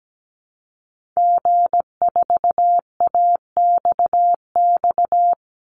Morse code, heard as a keyed tone, about 17 wpm, 700 Hz. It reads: G4AXX
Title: Call Signs - 17wpm